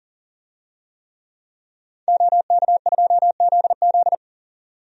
Morse code, heard as a keyed tone, about 40 wpm, 700 Hz. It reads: OK2ZZ
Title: DXCC-100 Call Signs - 5 Characters - 40wpm